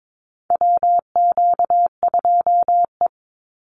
Morse code, heard as a keyed tone, about 22 wpm, 700 Hz. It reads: WQ2E
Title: Call Signs Repeated 2x - 22wpm